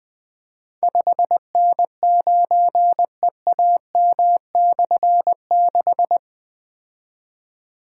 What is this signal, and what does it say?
Morse code, heard as a keyed tone, about 20 wpm, 700 Hz. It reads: 5N9EAM/6
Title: Difficult Call Signs 20wpm